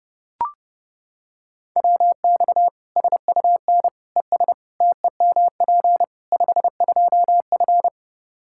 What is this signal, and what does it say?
Morse code, heard as a keyed tone, about 30 wpm, 700 Hz. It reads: WX SUN ES TEMP 52F
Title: Practice QSO - Exchange 2 - Info - 30wpm